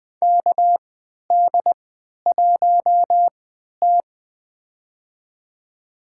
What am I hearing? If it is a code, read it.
Morse code, heard as a keyed tone, about 20 wpm, spaced as for 11 wpm, 700 Hz. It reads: KD1T
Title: USA Call Signs - 4 Characters - 3x Character Spacing - 20wpm